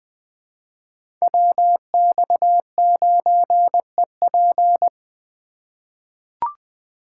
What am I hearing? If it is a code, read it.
Morse code, heard as a keyed tone, about 20 wpm, 700 Hz. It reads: WX9EP
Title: USA Call Signs - 5 Characters - 4x Character Spacing - 20wpm